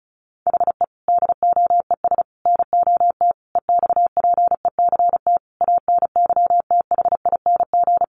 Morse code, heard as a keyed tone, about 34 wpm, 700 Hz. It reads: HE DOES NOT EXPECT ANYTHING
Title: Sentences from Top 1100 Words 35wpm